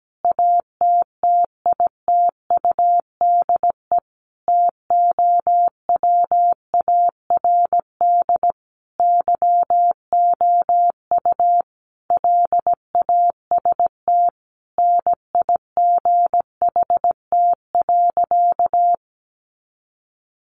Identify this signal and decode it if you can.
Morse code, heard as a keyed tone, about 17 wpm, 700 Hz. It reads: ATTITUDE TOWARD YOU LAST NIGHT.
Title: Sentences from Top 1300 Words 17wpm